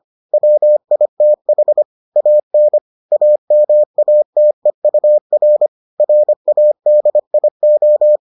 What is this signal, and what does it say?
Morse code, heard as a keyed tone, about 25 wpm, 600 Hz. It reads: WITH AN AMATEUR RADIO